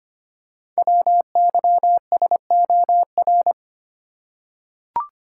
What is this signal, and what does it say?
Morse code, heard as a keyed tone, about 25 wpm, 700 Hz. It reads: WYSOR
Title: Code Groups - 5 Letters - 25wpm